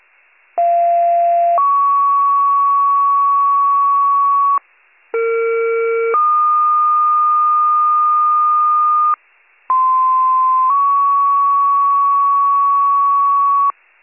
Two-tone page, heard as a one-second tone, then a three-second tone, about 690 then 1120 Hz, about 460 then 1220 Hz, about 1040 then 1120 Hz. It is heard from 0.6 to 13.7 s.